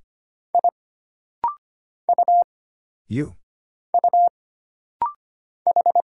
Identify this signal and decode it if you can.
Morse code, heard as a keyed tone, about 25 wpm, 700 Hz. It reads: I U U H
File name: Single Character (Letter:Number:Punctuation:Prosign) - Rapid-Fire - 25wpm